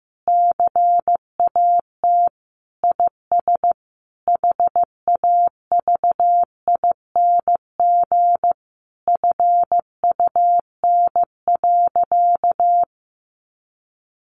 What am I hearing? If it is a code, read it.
Morse code, heard as a keyed tone, about 15 wpm, 700 Hz. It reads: CAT IS HAVING FUN.
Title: Easy Phrases 15wpm